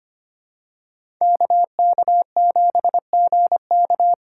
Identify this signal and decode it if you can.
Morse code, heard as a keyed tone, about 25 wpm, 700 Hz. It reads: KK7GK
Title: USA Callsign DE Callsign - 5 Characters - 25wpm